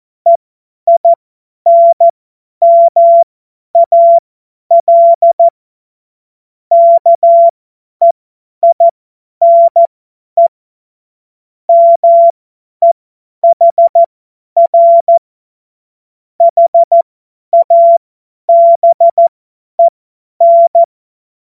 Morse code, heard as a keyed tone, about 14 wpm, spaced as for 10 wpm, 680 Hz. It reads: EINMAL KEINE MEHR HABEN
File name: Morsekurs / Witze / Witz 18. Juli 2023 50 F70